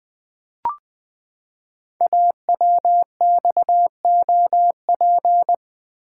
Morse code, heard as a keyed tone, about 20 wpm, 700 Hz. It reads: AWXOP
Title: Code Groups - 5 Letters - 20wpm